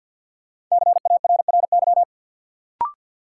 Morse code, heard as a keyed tone, about 52 wpm, 700 Hz. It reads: CARRY
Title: Sentences from Top 400 Words - New Word Drill - 50wpm